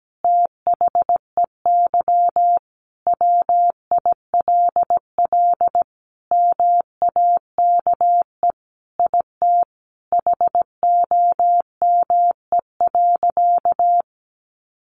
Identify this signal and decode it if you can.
Morse code, heard as a keyed tone, about 17 wpm, 700 Hz. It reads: THEY WILL MAKE IT HOME.
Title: Sentences from Top 1500 Words 17wpm